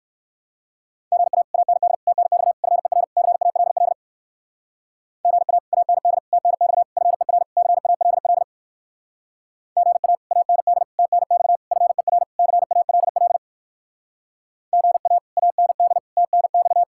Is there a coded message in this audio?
Yes: Morse code from 1.1 to 16.8 s, heard as a keyed tone, about 56 wpm, 700 Hz. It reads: GA AND TNX FER CALL GA AND TNX FER CALL GA AND TNX FER CALL GA AND TNX